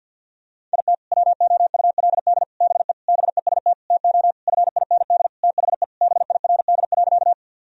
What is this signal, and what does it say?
Morse code, heard as a keyed tone, about 50 wpm, 700 Hz. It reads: IT WOULD BE BEST TO FIND THE BIRD.
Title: Sentences from Top 1000 Words - Review Part 2 - 50wpm